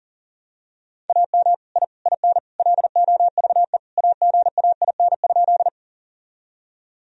Morse code, heard as a keyed tone, about 40 wpm, 700 Hz. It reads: AM I IN LOVE AGAIN?
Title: Sentences from Top 1000 Words - Review Part 2 - 40wpm